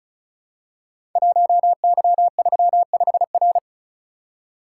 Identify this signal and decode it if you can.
Morse code, heard as a keyed tone, about 36 wpm, 690 Hz. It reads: 1Y35R
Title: Code Groups - 5 Letter:Numbers - 35wpm